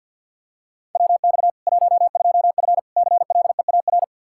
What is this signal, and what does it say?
Morse code, heard as a keyed tone, about 51 wpm, 700 Hz. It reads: WX 12F CLEAR
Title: Practice QSO - Exchange 2 - Info - 50wpm